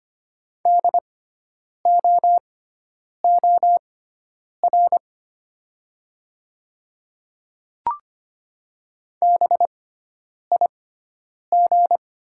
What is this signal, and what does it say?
Morse code, heard as a keyed tone, about 25 wpm, spaced as for 9 wpm, 690 Hz. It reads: DOOR BIG
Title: 6x Character Spacing - Top 300 Words - 25wpm